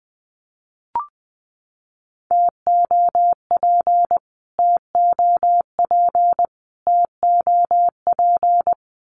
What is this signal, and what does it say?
Morse code, heard as a keyed tone, about 20 wpm, 700 Hz. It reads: TOP TOP TOP
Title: Top 500 Words Repeated 20wpm